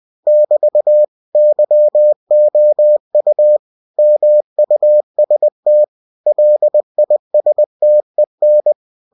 Morse code, heard as a keyed tone, about 20 wpm, 600 Hz. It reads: <BT> YOU MUST LISTEN